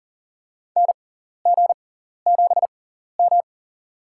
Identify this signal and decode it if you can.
Morse code, heard as a keyed tone, about 40 wpm, spaced as for 14 wpm, 700 Hz. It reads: NG7M
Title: USA Call Signs - 4 Characters - 6x Character Spacing - 40wpm